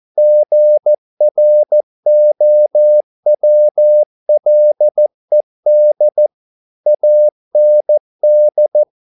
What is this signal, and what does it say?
Morse code, heard as a keyed tone, about 14 wpm, 600 Hz. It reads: GROWLED AND
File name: mp3 files / a dogs mistake - 14wpm